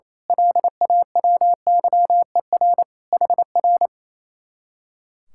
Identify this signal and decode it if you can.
Morse code, heard as a keyed tone, about 27 wpm, 700 Hz. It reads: LAWYER HR
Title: Speed-Racing Common QSO Phrases - 28wpm